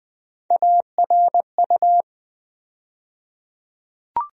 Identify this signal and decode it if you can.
Morse code, heard as a keyed tone, about 20 wpm, 700 Hz. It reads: ARU
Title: USA Call Sign Postfixes - 3 Letters 20wpm